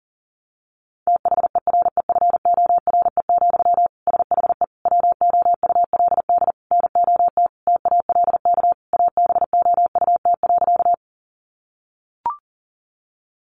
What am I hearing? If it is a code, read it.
Morse code, heard as a keyed tone, about 40 wpm, 700 Hz. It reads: THEREFORE, SHE WOULD NOT TALK ABOUT.
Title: Sentences from Top 800 Words 40wpm